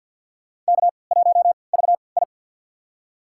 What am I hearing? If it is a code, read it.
Morse code, heard as a keyed tone, about 52 wpm, spaced as for 29 wpm, 700 Hz. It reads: K1VI
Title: USA Call Signs - 4 Characters - 3x Character Spacing - 50wpm